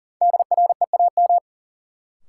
Morse code, heard as a keyed tone, about 40 wpm, 700 Hz. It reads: DREAM
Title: Sentences from Top 1500 Words - New Word Drill Alt - 40wpm